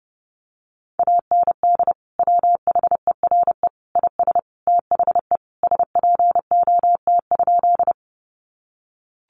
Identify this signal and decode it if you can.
Morse code, heard as a keyed tone, about 30 wpm, 700 Hz. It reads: AND WHERE IS THE SPOT?
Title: Sentences from Top 1100 Words 30wpm